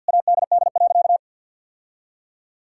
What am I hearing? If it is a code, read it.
Morse code, heard as a keyed tone, about 51 wpm, 700 Hz. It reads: ADD.
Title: Sentences from Top 1000 Words - Review Part 1 - 50wpm